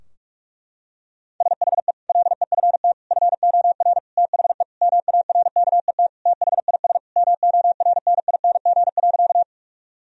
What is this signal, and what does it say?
Morse code, heard as a keyed tone, about 48 wpm, 700 Hz. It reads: SHE LEFT FOR THE MARKET THIS MORNING.
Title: Sentences from Top 1000 Words - Review Part 3 - 45wpm